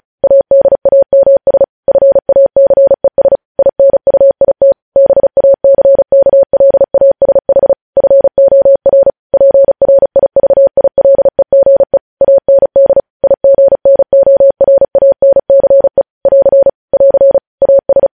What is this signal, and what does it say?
Morse code, heard as a keyed tone, about 36 wpm, 550 Hz. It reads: ADAMS FACES INUIT BACKLASH FOR PRIVILEGE AND IGNORANCE <AR> <AR> AS
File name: podcast / News headlines-1x-35x35